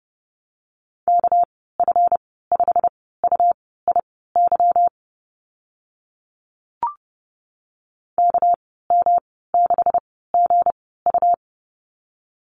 Morse code, heard as a keyed tone, about 30 wpm, spaced as for 17 wpm, 700 Hz. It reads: KF5UIY KM6GU
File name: Lesson 37g - Letter x - Call Signs 30wpm